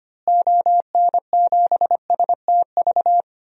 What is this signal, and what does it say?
Morse code, heard as a keyed tone, about 25 wpm, 700 Hz. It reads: ON7STV
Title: DXCC-100 Call Signs - 6 Characters - Repeated 2x - 25wpm